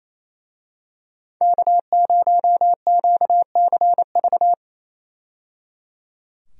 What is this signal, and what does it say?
Morse code, heard as a keyed tone, about 28 wpm, 700 Hz. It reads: K0QCV
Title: USA Call Signs - 5 Characters - 28wpm